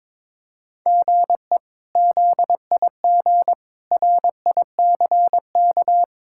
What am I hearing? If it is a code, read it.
Morse code, heard as a keyed tone, about 21 wpm, 700 Hz. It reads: GE ZIG RICK
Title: K1USN SST QSO Practice - 22wpm